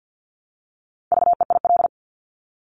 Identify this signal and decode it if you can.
Morse code, heard as a keyed tone, about 50 wpm, 690 Hz. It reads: VEIL
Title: Multi-letter Phonemes Repeated 3x 50wpm